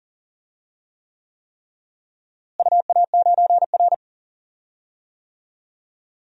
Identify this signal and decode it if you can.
Morse code, heard as a keyed tone, about 40 wpm, 700 Hz. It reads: UA9R
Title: DXCC-100 Call Signs - 4 Characters - Repeated 3x - 40wpm